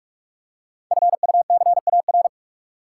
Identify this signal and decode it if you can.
Morse code, heard as a keyed tone, about 44 wpm, 700 Hz. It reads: FUCAR
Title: Code Groups - 5 Letter:Numbers - 45wpm